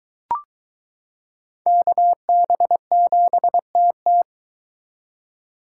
Morse code, heard as a keyed tone, about 21 wpm, 700 Hz. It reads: KB7TT